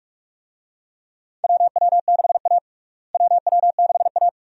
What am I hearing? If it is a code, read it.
Morse code, heard as a keyed tone, about 46 wpm, 700 Hz. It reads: WW6A WW6A
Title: USA Call Signs - 4 Characters - Repeated 2x - 45wpm